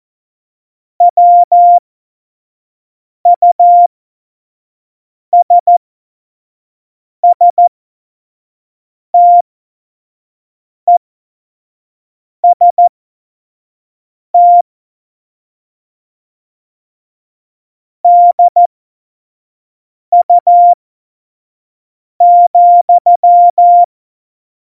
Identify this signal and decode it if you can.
Morse code, heard as a keyed tone, about 14 wpm, spaced as for 5 wpm, 700 Hz. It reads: WUSSTEST DU,
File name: Morsekurs / Witze / Witz 04. Februar 2020, 25 F70